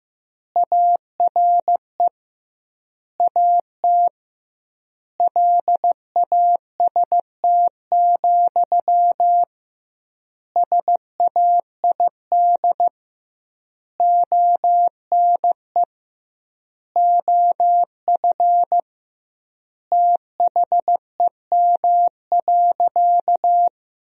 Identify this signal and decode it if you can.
Morse code, heard as a keyed tone, about 15 wpm, 700 Hz. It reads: ARE AT LAST, SAID ONE OF THEM.
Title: 2x Word-Spacing - Sentences from Top 300 Words - 15wpm